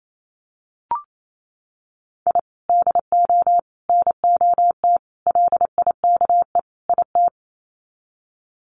Morse code, heard as a keyed tone, about 28 wpm, 700 Hz. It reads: I DO NOT LIKE IT
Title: Sets of 5 Words Encore 28wpm